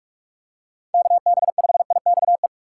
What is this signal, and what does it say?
Morse code, heard as a keyed tone, about 41 wpm, 700 Hz. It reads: KB5IXE
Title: USA Call Signs - 6 Characters - 2x Character Spacing - 45wpm